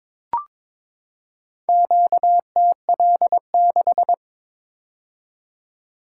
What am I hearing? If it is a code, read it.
Morse code, heard as a keyed tone, about 22 wpm, 700 Hz. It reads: QTL6
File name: practice / 22wpm / Four Letter-Number Combinations - 22wpm